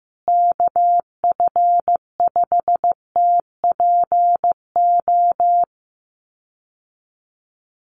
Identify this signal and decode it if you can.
Morse code, heard as a keyed tone, about 15 wpm, 700 Hz. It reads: KF5TPO
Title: USA Call Signs Repeated 2x 15wpm